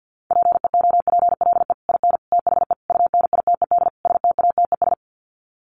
Morse code, heard as a keyed tone, about 48 wpm, 700 Hz. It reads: PEOPLE IN THE UNITED STATES
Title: Sets of 5 Words Encore 50wpm